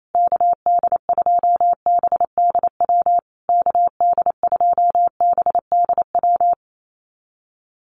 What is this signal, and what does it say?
Morse code, heard as a keyed tone, about 28 wpm, 700 Hz. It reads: KD2BDW KD2BDW
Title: USA Call Signs Repeated 3x 28wpm